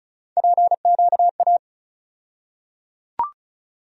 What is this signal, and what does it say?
Morse code, heard as a keyed tone, about 34 wpm, 700 Hz. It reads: PQA
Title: USA Call Sign Postfixes - 3 Letters 35wpm